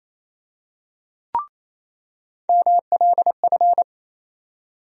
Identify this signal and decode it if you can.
Morse code, heard as a keyed tone, about 28 wpm, 700 Hz. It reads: MLF
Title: USA Call Sign Postfixes - 3 Letters 28wpm